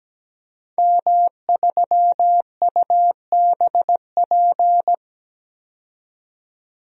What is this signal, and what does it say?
Morse code, heard as a keyed tone, about 17 wpm, 700 Hz. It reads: M3UBP
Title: DXCC-100 Call Signs - 5 Characters - 17wpm